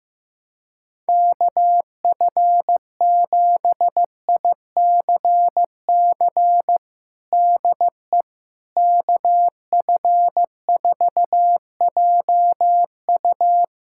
Morse code, heard as a keyed tone, about 15 wpm, 700 Hz. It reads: KF7ICC DE KF4JU
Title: USA Callsign DE Callsign 15wpm